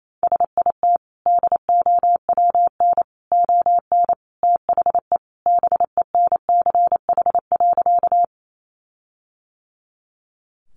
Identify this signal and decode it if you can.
Morse code, heard as a keyed tone, about 28 wpm, 700 Hz. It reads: SIT DOWN ON THE BENCH.
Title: Sentences from Top 1400 Words 28wpm